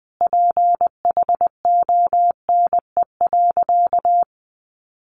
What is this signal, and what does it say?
Morse code, heard as a keyed tone, about 20 wpm, 700 Hz. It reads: PHONE.
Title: Sentences from Top 1100 Words 20wpm